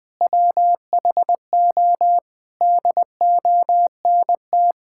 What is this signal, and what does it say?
Morse code, heard as a keyed tone, about 20 wpm, 700 Hz. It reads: WHO DONT
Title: Sentences with 7-Word Phrases - Spoken Beginning - 20wpm